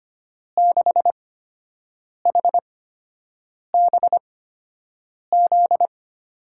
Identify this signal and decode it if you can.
Morse code, heard as a keyed tone, about 25 wpm, spaced as for 7 wpm, 700 Hz. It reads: 6HBZ
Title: USA Call Signs - 6 Characters - 8x Character Spacing - 25wpm